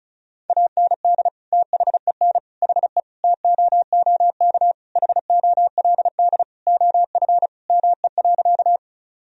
Morse code, heard as a keyed tone, about 34 wpm, 700 Hz. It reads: AND THEN HE TOOK HOLD OF ME.
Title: Sentences from Top 1000 Words - Review Part 2 - 35wpm